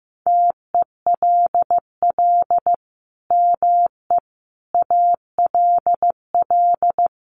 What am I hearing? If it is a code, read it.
Morse code, heard as a keyed tone, about 15 wpm, 700 Hz. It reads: TELL ME ALL
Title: Sentences from Top 800 Words 15wpm